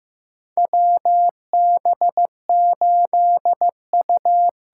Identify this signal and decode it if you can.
Morse code, heard as a keyed tone, about 15 wpm, 700 Hz. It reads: WB8U
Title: USA Call Signs - 6 Characters - 8x Character Spacing - 15wpm